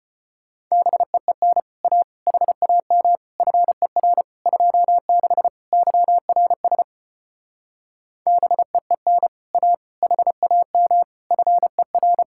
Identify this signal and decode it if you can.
Morse code, heard as a keyed tone, about 33 wpm, 700 Hz. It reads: BEEN A HAM FER 26 YRS BEEN A HAM FER